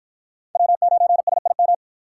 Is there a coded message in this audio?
Yes: Morse code from 0.6 to 1.7 s, heard as a keyed tone, about 50 wpm, 700 Hz. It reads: W9SIM